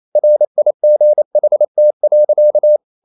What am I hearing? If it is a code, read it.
Morse code, heard as a keyed tone, about 28 wpm, 600 Hz. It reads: RIGHT.
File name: CW / Practice / EnglishSayings28